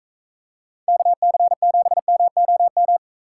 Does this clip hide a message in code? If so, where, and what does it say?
Yes: Morse code from 0.9 to 3.0 s, heard as a keyed tone, about 41 wpm, 700 Hz. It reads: KC7MOM